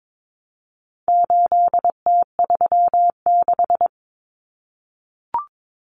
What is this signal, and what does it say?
Morse code, heard as a keyed tone, about 22 wpm, 700 Hz. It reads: 8T36
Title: Common Cut Numbers - 4-Digit 22wpm